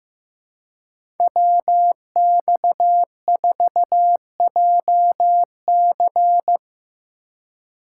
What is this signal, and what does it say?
Morse code, heard as a keyed tone, about 15 wpm, 700 Hz. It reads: WX4JC
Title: USA Call Signs - 5 Characters - 4x Character Spacing - 15wpm